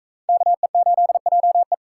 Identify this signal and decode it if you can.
Morse code, heard as a keyed tone, about 40 wpm, 700 Hz. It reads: KE8JE